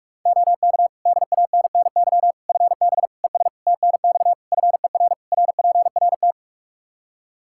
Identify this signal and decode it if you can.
Morse code, heard as a keyed tone, about 46 wpm, 700 Hz. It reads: OK DANNY FB ES TNX FER RPRT
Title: Practice QSO - Exchange 2 - Info - 45wpm